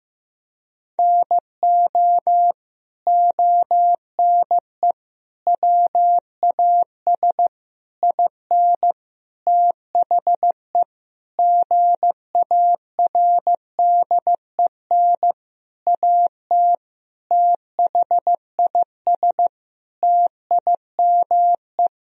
Morse code, heard as a keyed tone, about 15 wpm, 700 Hz. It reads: NO ONE WAS IN THE GARDEN AT THIS TIME
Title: Sentences from Top 1000 Words - Review Part 4 - 15wpm